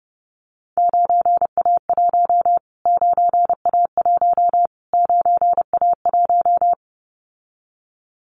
Morse code, heard as a keyed tone, about 30 wpm, 700 Hz. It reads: 9A1 9A1 9A1
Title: Callsign Prefixes for Top 25 DXCC - Repeated 3x - 30wpm